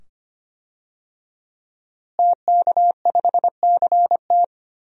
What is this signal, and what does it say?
Morse code, heard as a keyed tone, about 25 wpm, 700 Hz. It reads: TK5CT
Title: DXCC-100 Call Signs - 5 Characters - Repeated 2x - 25wpm